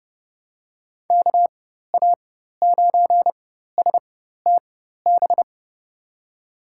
Morse code, heard as a keyed tone, about 30 wpm, spaced as for 14 wpm, 700 Hz. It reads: KA9STB
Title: USA Call Signs - 4x Character Spacing - 30wpm